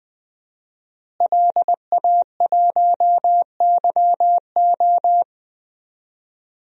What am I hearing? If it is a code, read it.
Morse code, heard as a keyed tone, about 20 wpm, 700 Hz. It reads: LA1YO
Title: DXCC-100 Call Signs - 5 Characters - Repeated 3x - 20wpm